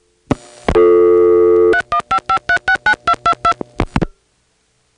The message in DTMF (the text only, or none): 3155369222